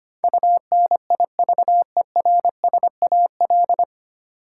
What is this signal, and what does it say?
Morse code, heard as a keyed tone, about 25 wpm, 700 Hz. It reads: UNIVERSAL
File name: Nine Letter Words - 25wpm